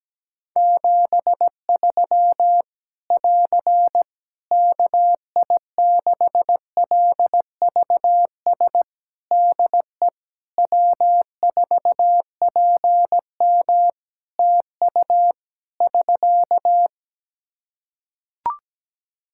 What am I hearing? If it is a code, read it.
Morse code, heard as a keyed tone, about 17 wpm, 700 Hz. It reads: 73 <AR> KI6LVS DE W4PM TU <SK>
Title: Practice QSO - Exchange 4 - Ending - 17wpm